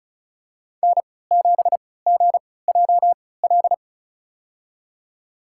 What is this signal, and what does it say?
Morse code, heard as a keyed tone, about 34 wpm, spaced as for 20 wpm, 700 Hz. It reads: N7GJL
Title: USA Call Signs - 5 Characters - 3x Character Spacing - 35wpm